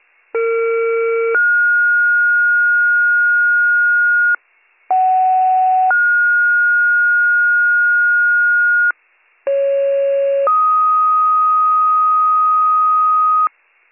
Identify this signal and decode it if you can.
Two-tone page, heard as a one-second tone, then a three-second tone, about 460 then 1430 Hz, about 750 then 1430 Hz, about 550 then 1250 Hz, with noise. It is heard from 0.3 to 13.5 s.